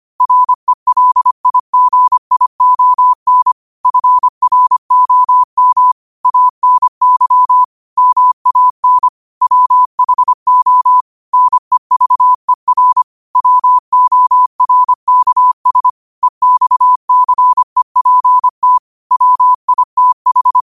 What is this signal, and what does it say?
Morse code, heard as a keyed tone, about 25 wpm, 1000 Hz. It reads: RELIGION FROM ANY MAN WHO NEVER WORKS EXCEPT WITH
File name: qotdfiles / 25wpm